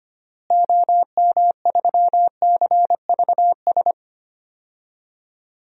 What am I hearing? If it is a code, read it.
Morse code, heard as a keyed tone, about 25 wpm, 700 Hz. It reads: OM3CVS
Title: DXCC-100 Call Signs - 6 Characters - Repeated 3x - 25wpm